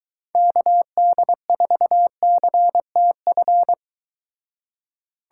Morse code, heard as a keyed tone, about 23 wpm, 700 Hz. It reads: KD4CTF